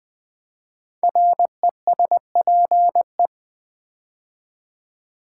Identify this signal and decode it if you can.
Morse code, heard as a keyed tone, about 21 wpm, 700 Hz. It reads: RESPE
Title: Build A Word - 5 Letters At A Time - 12-Letter Words - 20wpm